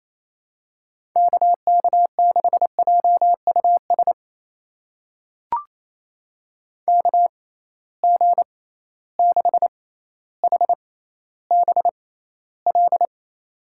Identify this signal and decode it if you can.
Morse code, heard as a keyed tone, about 28 wpm, spaced as for 14 wpm, 700 Hz. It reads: KK6JUS KG6HBL
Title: USA Call Signs - 6x Character Spacing - 28wpm